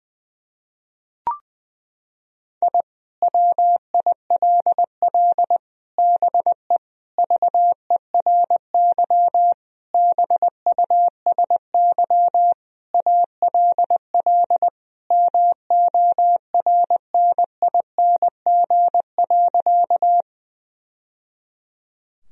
Morse code, heard as a keyed tone, about 20 wpm, 700 Hz. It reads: I WILL BE VERY BUSY ALL MORNING.
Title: Sentences from Top 1000 Words - Review Part 3 - 20wpm